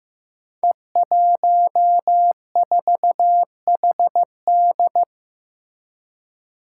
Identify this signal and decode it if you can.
Morse code, heard as a keyed tone, about 15 wpm, 700 Hz. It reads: E14HD
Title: Five Challenging Letter-Number Combinations 15wpm